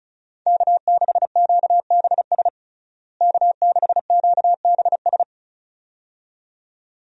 Morse code, heard as a keyed tone, about 36 wpm, 700 Hz. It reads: K6QBS K6QBS
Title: USA Call Signs - 5 Characters - Repeated 2x - 35wpm